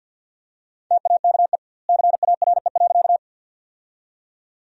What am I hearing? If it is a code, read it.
Morse code, heard as a keyed tone, about 50 wpm, 700 Hz. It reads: TAKE CARE.
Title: 2x Word-Spacing - Sentences from Top 300 Words - 50wpm